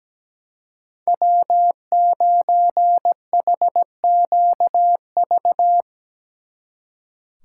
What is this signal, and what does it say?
Morse code, heard as a keyed tone, about 19 wpm, 700 Hz. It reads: W9HQV
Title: Speed Racing - USA Call Signs 17wpm